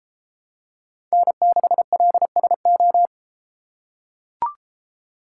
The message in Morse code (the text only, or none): N6LSO